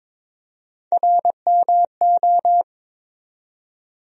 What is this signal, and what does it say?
Morse code, heard as a keyed tone, about 22 wpm, 700 Hz. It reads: RMO
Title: USA Call Sign Postfixes - 3 Letters 22wpm